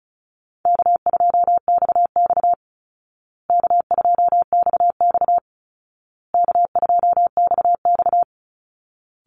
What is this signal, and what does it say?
Morse code, heard as a keyed tone, about 34 wpm, 700 Hz. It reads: K2XX K2XX K2XX
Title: Easy Call Signs Repeated 3x - 35wpm